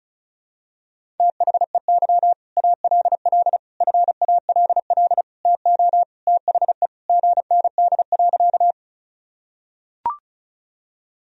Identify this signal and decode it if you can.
Morse code, heard as a keyed tone, about 34 wpm, 700 Hz. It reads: THEY ALL FALL TO THE GND.
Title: Sentences from Top 500 Words and Abbreviations 35wpm